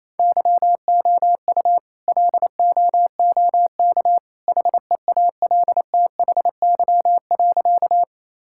Morse code, heard as a keyed tone, about 27 wpm, 700 Hz. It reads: YOU LOOK HEALTHY.
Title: Sentences from Top 1000 Words - Review Part 5 - 28wpm